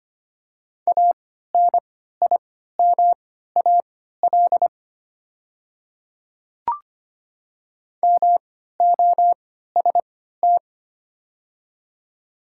Morse code, heard as a keyed tone, about 25 wpm, spaced as for 14 wpm, 700 Hz. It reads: ANIMAL MOST
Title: 3x Character Spacing - Top 500 Words - 25wpm